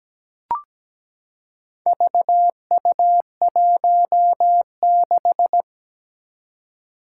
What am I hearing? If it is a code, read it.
Morse code, heard as a keyed tone, about 17 wpm, 700 Hz. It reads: VU16
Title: Four Challenging Letter-Number Combinations 17wpm